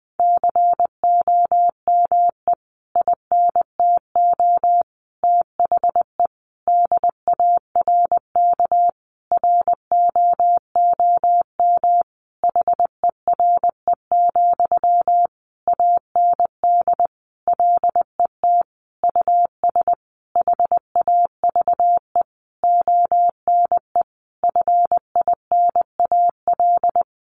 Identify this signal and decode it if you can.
Morse code, heard as a keyed tone, about 20 wpm, 700 Hz. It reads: COME INTO THE DARK ROOM HERE, AND LET US HAVE ONE FINAL
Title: Sentences from Top 1000 Words 20wpm